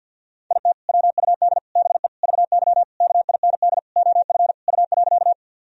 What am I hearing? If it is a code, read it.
Morse code, heard as a keyed tone, about 51 wpm, 700 Hz. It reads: IT WUD BE VY KIND OF U.
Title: Sentences from Top 200 Words and Abbreviations 50wpm